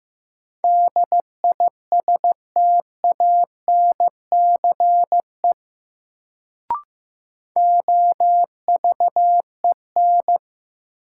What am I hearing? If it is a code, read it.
Morse code, heard as a keyed tone, about 15 wpm, 700 Hz. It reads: DISTANCE OVEN
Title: Sentences from Top 1200 Words - New Word Drill Alt - 15wpm